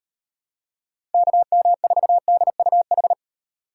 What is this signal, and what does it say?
Morse code, heard as a keyed tone, about 36 wpm, 700 Hz. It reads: KM4DUH